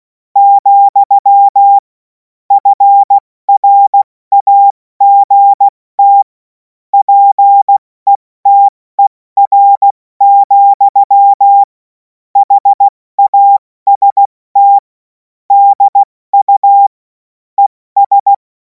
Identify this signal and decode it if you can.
Morse code, heard as a keyed tone, about 16 wpm, spaced as for 14 wpm, 800 Hz. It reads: , FRAGT PETER, HAST DU ES
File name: Morsekurs / Witze / Witz 07. April 2026 70 F80